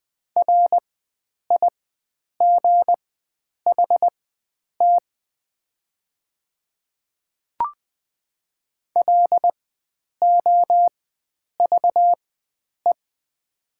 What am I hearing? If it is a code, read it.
Morse code, heard as a keyed tone, about 20 wpm, spaced as for 9 wpm, 700 Hz. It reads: RIGHT LOVE
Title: 4x Character Spacing - Top 500 Words - 20wpm